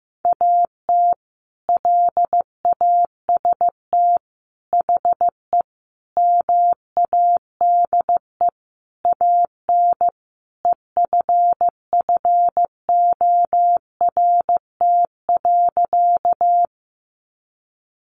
Morse code, heard as a keyed tone, about 15 wpm, 700 Hz. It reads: AT LAST HE MADE AN EFFORT.
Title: Sentences from Top 1200 Words 15wpm